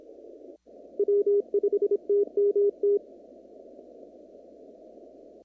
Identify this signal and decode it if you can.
Morse code, heard as a keyed tone, about 26 wpm, 410 Hz. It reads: W5TMT
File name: CW / kt5j wpx cw-op-k5pi